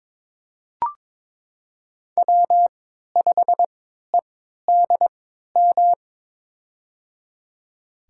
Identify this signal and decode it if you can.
Morse code, heard as a keyed tone, about 22 wpm, spaced as for 12 wpm, 700 Hz. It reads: W5EDM
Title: USA Call Signs - 5 Characters - 3x Character Spacing - 22wpm